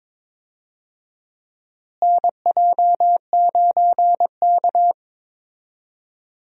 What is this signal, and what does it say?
Morse code, heard as a keyed tone, about 21 wpm, 700 Hz. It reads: NJ9K
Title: USA Call Signs - 4 Characters - Repeated 2x - 22wpm